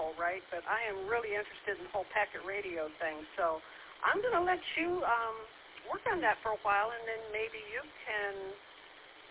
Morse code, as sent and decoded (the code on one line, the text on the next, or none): none